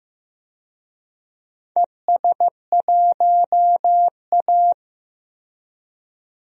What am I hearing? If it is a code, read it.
Morse code, heard as a keyed tone, about 15 wpm, 700 Hz. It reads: ES1A
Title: DXCC-100 Call Signs - 4 Characters - 15wpm